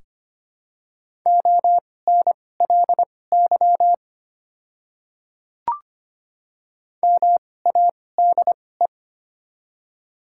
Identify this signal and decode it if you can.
Morse code, heard as a keyed tone, about 25 wpm, spaced as for 18 wpm, 700 Hz. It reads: ONLY MADE
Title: 2x Character Spacing - Top 300 Words - 25wpm